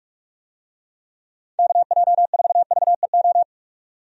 Morse code, heard as a keyed tone, about 42 wpm, 700 Hz. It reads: KJ4VEO
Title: USA Call Signs - 6 Characters - 45wpm